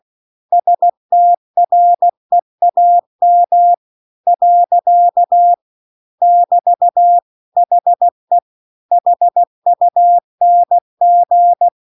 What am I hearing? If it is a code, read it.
Morse code, heard as a keyed tone, about 16 wpm, 700 Hz. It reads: STREAM . <BT> HE HUNG
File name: CW-The-Big-Twoheaded-River-2-16WPM